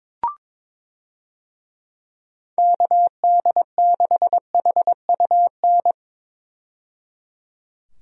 Morse code, heard as a keyed tone, about 21 wpm, 700 Hz. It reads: KD6HUN
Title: USA Call Signs - 6 Characters - 22wpm